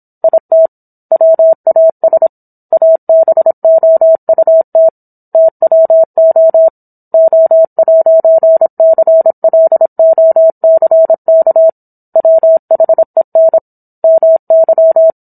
Morse code, heard as a keyed tone, about 26 wpm, 650 Hz. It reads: IT WAS ABOUT TWO O'CLOCK WHEN MY